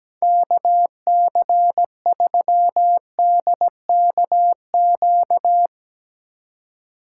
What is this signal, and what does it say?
Morse code, heard as a keyed tone, about 17 wpm, 700 Hz. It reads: KC3DKQ
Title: Speed Racing - USA 2x3 Call Signs 17wpm